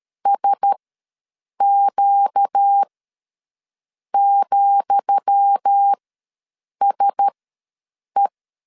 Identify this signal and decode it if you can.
Morse code, heard as a keyed tone, about 13 wpm, spaced as for 7 wpm, 800 Hz. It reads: SQ,SE
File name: LEZIONI / 10 SIGNS / 10D COMA ws 12wpm 3618